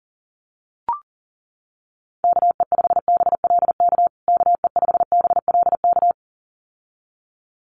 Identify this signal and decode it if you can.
Morse code, heard as a keyed tone, about 40 wpm, 700 Hz. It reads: KE5BLK KE5BLK
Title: USA Call Signs Repeated 2x 40wpm